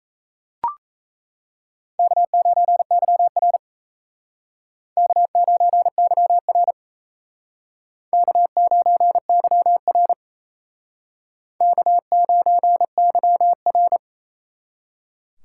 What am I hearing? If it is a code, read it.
Morse code, heard as a keyed tone, about 36 wpm, 700 Hz. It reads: K9YR K9YR K9YR K9YR